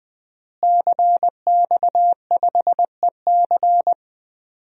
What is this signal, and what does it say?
Morse code, heard as a keyed tone, about 21 wpm, 700 Hz. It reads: CX5EC
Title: DXCC-100 Call Signs - 5 Characters - Repeated 3x - 20wpm